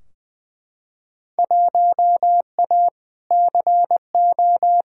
Morse code, heard as a keyed tone, about 20 wpm, 700 Hz. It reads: 1A CO
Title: Field Day Exchanges - 20wpm